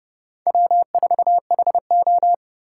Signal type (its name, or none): Morse code